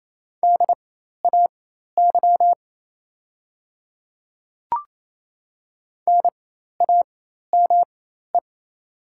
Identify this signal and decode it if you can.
Morse code, heard as a keyed tone, about 28 wpm, spaced as for 13 wpm, 700 Hz. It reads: DAY NAME
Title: 4x Character Spacing - Top 500 Words - 28wpm